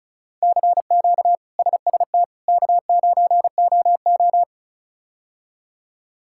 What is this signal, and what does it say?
Morse code, heard as a keyed tone, about 36 wpm, 700 Hz. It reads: CQ SST K9OO
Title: K1USN SST QSO Practice - 35wpm